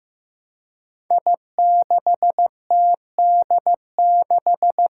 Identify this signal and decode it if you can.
Morse code, heard as a keyed tone, about 15 wpm, 700 Hz. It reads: I6TD6
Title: Five Challenging Letter-Number Combinations 15wpm